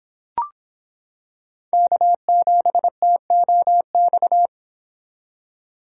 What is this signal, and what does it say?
Morse code, heard as a keyed tone, about 26 wpm, 700 Hz. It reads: K7TOX